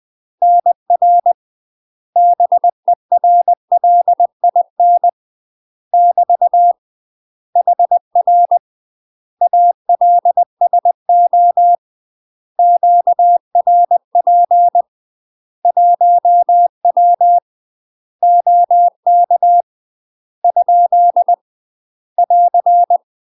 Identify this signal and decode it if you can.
Morse code, heard as a keyed tone, about 20 wpm, 700 Hz. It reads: NR BERLIN <BT> HR ALSO QRP 1W OK ? <AR>
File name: UK MASTER QSOs TRAINING / 20WPM / UK MASTER QSOs 0040